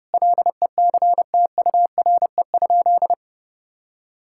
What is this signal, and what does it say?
Morse code, heard as a keyed tone, about 32 wpm, 700 Hz. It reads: LECTURE?
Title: Sentences from Top 1600 Words 30wpm